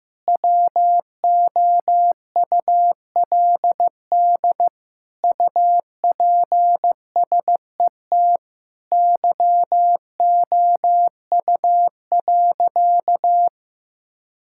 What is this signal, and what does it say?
Morse code, heard as a keyed tone, about 15 wpm, 700 Hz. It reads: WOULD UPSET YOU.
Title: Sentences from Top 1000 Words - Review Part 5 - 15wpm